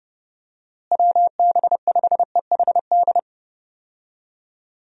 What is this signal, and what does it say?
Morse code, heard as a keyed tone, about 30 wpm, 700 Hz. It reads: WB5EHD
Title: USA Call Signs - 6 Characters - Repeated 2x - 30wpm